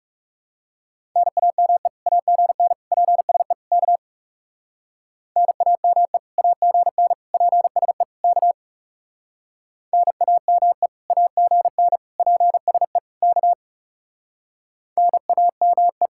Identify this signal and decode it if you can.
Morse code, heard as a keyed tone, about 35 wpm, 700 Hz. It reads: NAME AGN PSE K NAME AGN PSE K NAME AGN PSE K NAME
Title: Speed-Racing Common QSO Phrases Encore - 30wpm